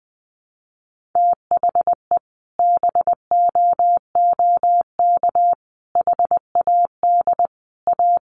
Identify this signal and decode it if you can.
Morse code, heard as a keyed tone, about 20 wpm, 700 Hz. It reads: THE BOOK HAD A
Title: Sentences from Top 1400 Words 20wpm